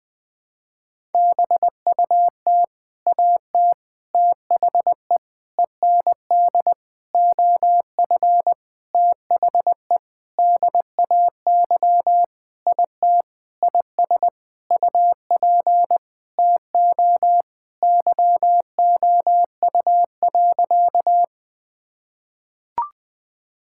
Morse code, heard as a keyed tone, about 20 wpm, 700 Hz. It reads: BUT AT THE END OF THE DAY IT IS UP TO YOU.
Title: Sentences with 7-Word Phrases - 20wpm